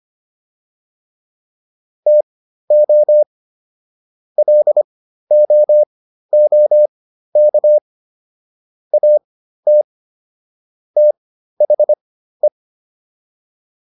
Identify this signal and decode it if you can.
Morse code, heard as a keyed tone, about 25 wpm, spaced as for 13 wpm, 600 Hz. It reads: TO LOOK AT THE